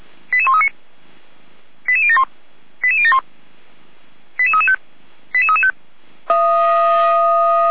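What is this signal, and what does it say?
Five-tone selective call (ZVEI), heard as quick runs of five tones, tones of about 70 ms. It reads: 88139 88071 88071 88335 88335
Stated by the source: ZVEI five-tone selective call 88090 received by SDR.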